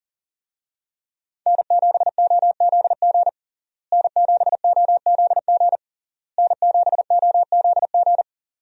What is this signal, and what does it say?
Morse code, heard as a keyed tone, about 40 wpm, 700 Hz. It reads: N7OZG N7OZG N7OZG
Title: USA Call Signs - 5 Characters - Repeated 3x - 40wpm